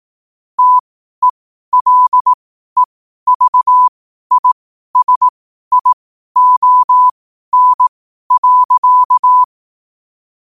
Morse code, heard as a keyed tone, about 18 wpm, spaced as for 12 wpm, 1000 Hz. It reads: TELEVISION.
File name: qotdfiles / 12wpm